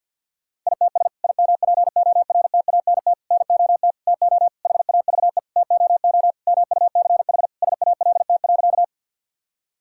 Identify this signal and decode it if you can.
Morse code, heard as a keyed tone, about 50 wpm, 700 Hz. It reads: ITS IMPORTANT NOT TO HAVE TOO MUCH SALT.
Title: Sentences from Top 1000 Words - Review Part 5 - 50wpm